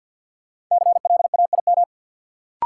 Morse code, heard as a keyed tone, about 49 wpm, 700 Hz. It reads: CLAIM
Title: Sentences from Top 1500 Words - New Word Drill Alt - 50wpm